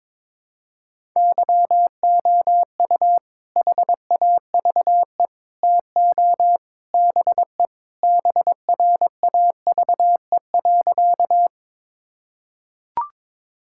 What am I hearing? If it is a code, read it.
Morse code, heard as a keyed tone, about 22 wpm, 700 Hz. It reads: YOU HAVE TO BE BRAVE.
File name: practice / 22wpm / Sentences with 3-Word Phrases - 22wpm